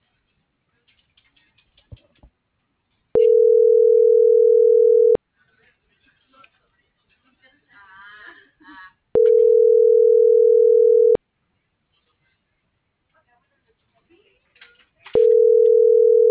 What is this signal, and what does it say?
Ringback tone (North America), 440 + 480 Hz.